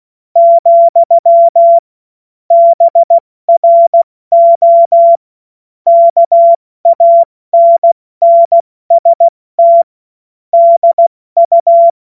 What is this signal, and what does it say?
Morse code, heard as a keyed tone, about 16 wpm, 680 Hz. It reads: , BRO KANNST DU
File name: Morsekurs / Witze / Witz 10. August 2021, 70 F80